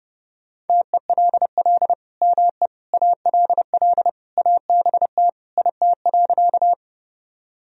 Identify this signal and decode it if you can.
Morse code, heard as a keyed tone, about 32 wpm, 700 Hz. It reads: TELL ME ALL ABT IT.
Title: Sentences from Top 200 Words and Abbreviations 30wpm